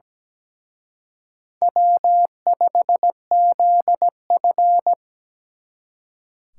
Morse code, heard as a keyed tone, about 20 wpm, 700 Hz. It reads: W5ZF
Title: Speed Racing - USA 1x2 Call Signs 17wpm